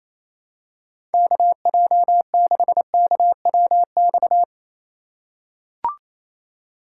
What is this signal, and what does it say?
Morse code, heard as a keyed tone, about 27 wpm, 700 Hz. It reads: KJ6KWX
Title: USA Call Signs - 6 Characters - 4x Character Spacing - 28wpm